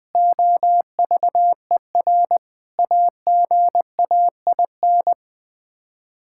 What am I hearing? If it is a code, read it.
Morse code, heard as a keyed tone, about 20 wpm, 700 Hz. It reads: OVER AGAIN
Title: Sets of 8 Words - 20wpm